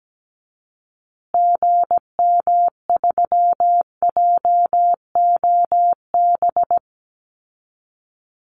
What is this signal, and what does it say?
Morse code, heard as a keyed tone, about 17 wpm, 700 Hz. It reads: GM3JOB
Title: Call Signs Repeated 2x - 17wpm